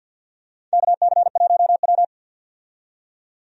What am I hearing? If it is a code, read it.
Morse code, heard as a keyed tone, about 53 wpm, 700 Hz. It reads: KC1W
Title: USA Call Signs - 4 Characters - 50wpm